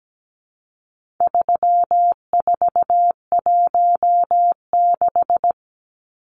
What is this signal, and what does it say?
Morse code, heard as a keyed tone, about 17 wpm, 700 Hz. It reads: 3416
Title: Numbers - 4-Digit 17wpm